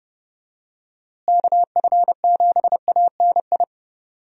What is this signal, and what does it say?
Morse code, heard as a keyed tone, about 30 wpm, 700 Hz. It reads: KF7ANI
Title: USA Call Signs - 6 Characters - Repeated 2x - 30wpm